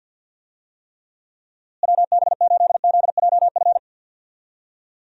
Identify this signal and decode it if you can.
Morse code, heard as a keyed tone, about 50 wpm, 700 Hz. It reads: WB8ZJF
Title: USA Call Signs - 6 Characters - Repeated 3x - 50wpm